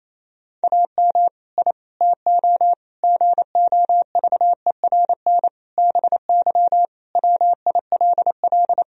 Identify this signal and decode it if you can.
Morse code, heard as a keyed tone, about 28 wpm, 700 Hz. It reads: AM I TO GOVERN BY WILL
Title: Sentences from Top 1000 Words - Review Part 2 - 28wpm